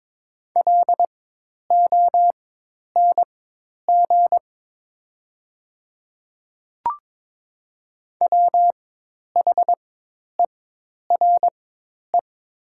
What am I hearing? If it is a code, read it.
Morse code, heard as a keyed tone, about 22 wpm, spaced as for 10 wpm, 700 Hz. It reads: LONG WHERE
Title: 4x Character Spacing - Top 300 Words - 22wpm